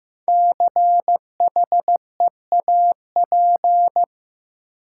Morse code, heard as a keyed tone, about 15 wpm, 700 Hz. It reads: CHEAP
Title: Sentences from Top 1200 Words - New Word Drill - 15wpm